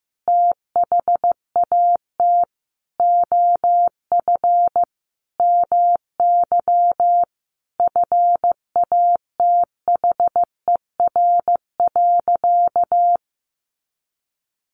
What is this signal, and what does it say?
Morse code, heard as a keyed tone, about 15 wpm, 700 Hz. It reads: THAT OF MY FATHER.
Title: Sentences from Top 500 Words 15wpm